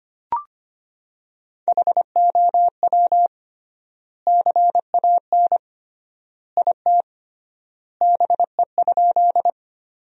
Morse code, heard as a keyed tone, about 25 wpm, 700 Hz. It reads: HOW CAN IT BE?
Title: 3x Word-Spacing - Sentences from Top 100 Words - 25wpm